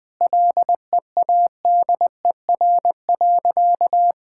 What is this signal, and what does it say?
Morse code, heard as a keyed tone, about 19 wpm, 700 Hz. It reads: LEADER.
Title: Sentences with 6-Word Phrases - 20wpm